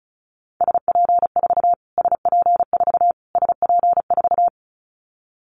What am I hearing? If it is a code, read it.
Morse code, heard as a keyed tone, about 34 wpm, 700 Hz. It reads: SP4 SP4 SP4
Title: Callsign Prefixes for Top 25 DXCC - Repeated 3x - 35wpm